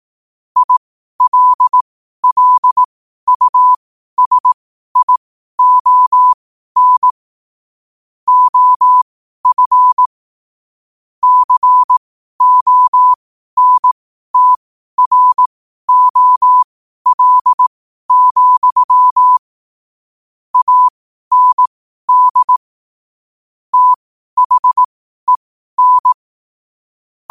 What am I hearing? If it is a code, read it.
Morse code, heard as a keyed tone, about 18 wpm, spaced as for 12 wpm, 1000 Hz. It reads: ILLUSION OF CONTROL, AND THEN
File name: qotdfiles / 12wpm